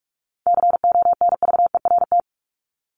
Morse code, heard as a keyed tone, about 44 wpm, 700 Hz. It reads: CONVERT
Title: Top 1751-2000 Words 45wpm